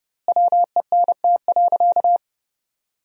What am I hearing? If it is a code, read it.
Morse code, heard as a keyed tone, about 30 wpm, 700 Hz. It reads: WENT.
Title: 4x Word-Spacing - Sentences from Top 200 Words - 30wpm